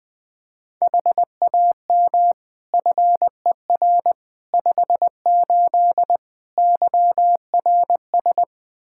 Morse code, heard as a keyed tone, about 20 wpm, 700 Hz. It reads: HAM FER 58 YRS
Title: Practice QSO - Exchange 3 - More Info - 20wpm